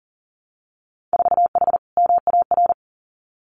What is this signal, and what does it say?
Morse code, heard as a keyed tone, about 37 wpm, 700 Hz. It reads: 4H MAR
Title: Winter Field Day Exchanges Encore 40wpm